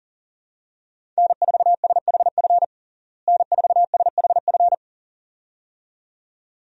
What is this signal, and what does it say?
Morse code, heard as a keyed tone, about 40 wpm, 700 Hz. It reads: N4SHF N4SHF
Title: USA Call Signs - 5 Characters - Repeated 2x - 40wpm